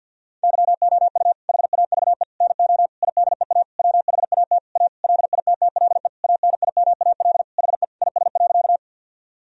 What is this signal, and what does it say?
Morse code, heard as a keyed tone, about 53 wpm, 700 Hz. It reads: YOU HAVE NO IDEA WHAT A LITTLE ANIMAL HE IS.
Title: Sentences from Top 1000 Words - Review Part 1 - 50wpm